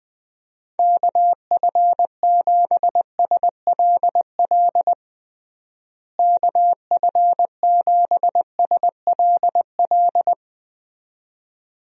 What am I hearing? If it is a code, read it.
Morse code, heard as a keyed tone, about 20 wpm, 700 Hz. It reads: KF7SLL KF7SLL
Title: USA Call Signs - 6 Characters - Repeated 2x - 20wpm